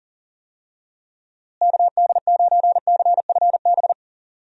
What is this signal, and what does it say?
Morse code, heard as a keyed tone, about 40 wpm, 700 Hz. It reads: KD9CFB
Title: USA Call Signs - 6 Characters - Repeated 2x - 40wpm